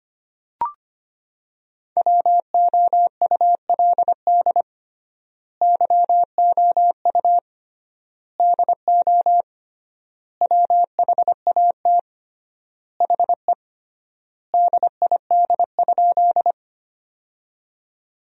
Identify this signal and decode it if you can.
Morse code, heard as a keyed tone, about 25 wpm, 700 Hz. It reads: WOULD YOU DO WHAT HE DID?
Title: 3x Word-Spacing - Sentences from Top 100 Words - 25wpm